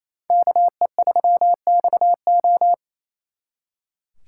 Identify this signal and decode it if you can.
Morse code, heard as a keyed tone, about 34 wpm, 700 Hz. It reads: KE3XO
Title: Speed Racing - USA 2x2 Call Signs 28wpm